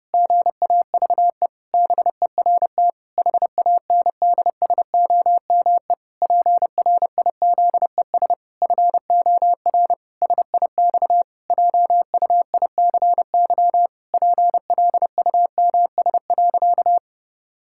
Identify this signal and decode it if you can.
Morse code, heard as a keyed tone, about 32 wpm, 700 Hz. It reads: GAVE BERT HANDSOME PRIZES FOR SIX JUICY PLUMS.
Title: Pangram Challenge v2 - 30wpm